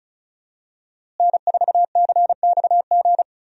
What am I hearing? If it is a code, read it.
Morse code, heard as a keyed tone, about 36 wpm, 700 Hz. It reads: N4CXG
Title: USA Call Signs - 5 Characters - Repeated 3x - 35wpm